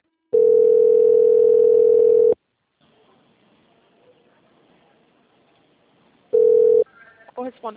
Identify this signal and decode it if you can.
Ringback tone (North America), 440 + 480 Hz.
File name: out- - - - - .1575